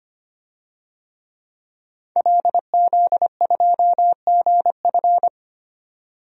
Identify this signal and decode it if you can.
Morse code, heard as a keyed tone, about 25 wpm, 700 Hz. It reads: LZ2GF
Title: DXCC-100 Call Signs - 5 Characters - 25wpm